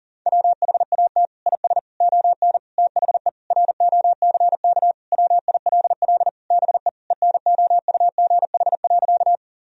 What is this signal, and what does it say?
Morse code, heard as a keyed tone, about 40 wpm, 700 Hz. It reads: WHAT IS ON THE ROCK WILL BE ENOUGH.
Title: Sentences from Top 1000 Words - Review Part 2 - 40wpm